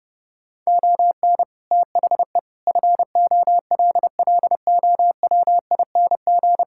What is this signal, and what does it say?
Morse code, heard as a keyed tone, about 30 wpm, 700 Hz. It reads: ON THE FOLLOWING
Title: Sentences from Top 1000 Words - Review Part 6 - 30wpm